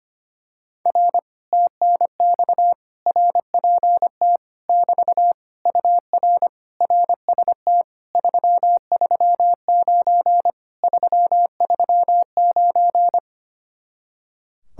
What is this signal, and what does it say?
Morse code, heard as a keyed tone, about 24 wpm, 700 Hz. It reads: R TNX RPT <BT> UR RST 339 339
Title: Speed-Racing Common QSO Phrases Encore - 25wpm